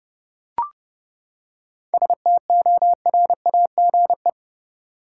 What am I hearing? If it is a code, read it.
Morse code, heard as a keyed tone, about 30 wpm, 700 Hz. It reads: STORAGE
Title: Seven Letter Words - 30wpm